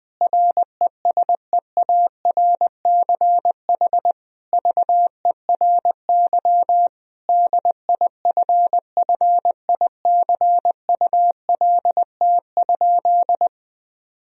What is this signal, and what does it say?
Morse code, heard as a keyed tone, about 20 wpm, 700 Hz. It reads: RESEARCH VERY DIFFICULT?
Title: Sentences from Top 1000 Words - Review Part 4 - 20wpm